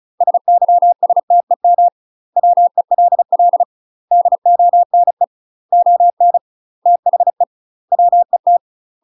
Morse code, heard as a keyed tone, about 36 wpm, 700 Hz. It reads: SYSTEM WELL DONE ON THE WET
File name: UK MASTER QSOs TRAINING / 35WPM / UK MASTER QSOs 0023